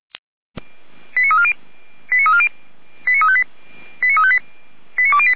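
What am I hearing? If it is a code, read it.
Five-tone selective call (ZVEI), heard as quick runs of five tones, tones of about 70 ms. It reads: 89340 89358 89358 89208
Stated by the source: ZVEI five-tone selective call 89330 received on an SDR.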